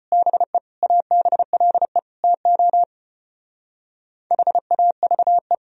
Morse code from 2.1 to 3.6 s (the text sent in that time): TO